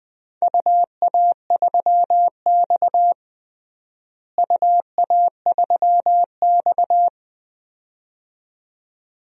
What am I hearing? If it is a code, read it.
Morse code, heard as a keyed tone, about 21 wpm, 700 Hz. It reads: UA3X UA3X
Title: DXCC-100 Call Signs - 4 Characters - Repeated 2x - 20wpm